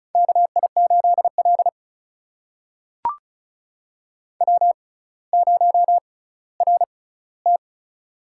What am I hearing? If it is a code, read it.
Morse code, heard as a keyed tone, about 36 wpm, spaced as for 18 wpm, 700 Hz. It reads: KI8L W0RT
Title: USA Call Signs - 4 Characters - 6x Character Spacing - 35wpm